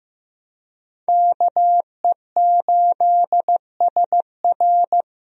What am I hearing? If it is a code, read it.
Morse code, heard as a keyed tone, about 14 wpm, 700 Hz. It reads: KE8SR
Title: USA Call Signs - 5 Characters - Repeated 3x - 15wpm